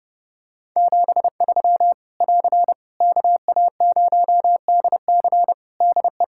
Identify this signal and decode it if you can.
Morse code, heard as a keyed tone, about 32 wpm, 700 Hz. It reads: 73 <AR> KA0DC DE
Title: K1USN SST QSO Practice 30wpm